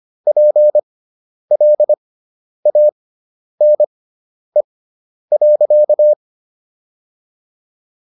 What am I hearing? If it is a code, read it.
Morse code, heard as a keyed tone, about 25 wpm, spaced as for 10 wpm, 600 Hz. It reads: PLANE.